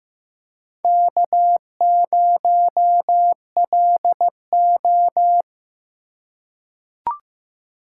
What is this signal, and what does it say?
Morse code, heard as a keyed tone, about 15 wpm, 700 Hz. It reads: K0LO
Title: USA Call Signs - 2x Character Spacing - 15wpm